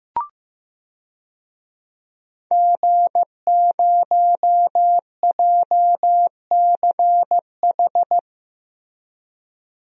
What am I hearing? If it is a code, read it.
Morse code, heard as a keyed tone, about 15 wpm, 700 Hz. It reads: G0JCH
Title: DXCC-100 Call Signs - 5 Characters - Repeated 2x - 15wpm